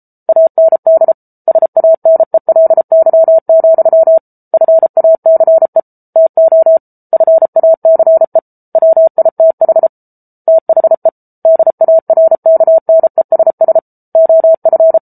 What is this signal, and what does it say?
Morse code, heard as a keyed tone, about 34 wpm, 650 Hz. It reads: AND SANELY, FACE TO FACE WITH THE DARKNESS OF